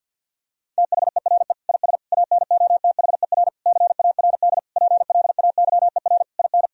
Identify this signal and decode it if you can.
Morse code, heard as a keyed tone, about 50 wpm, 700 Hz. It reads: THERE IS ANOTHER CARD PLAYER IN
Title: Sentences from Top 1000 Words - Review Part 4 - 50wpm